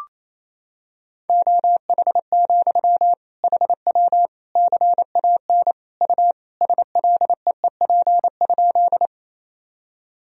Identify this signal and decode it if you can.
Morse code, heard as a keyed tone, about 28 wpm, 700 Hz. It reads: OH, HW CAN U SLEEP?
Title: Sentences from Top 500 Words and Abbreviations 28wpm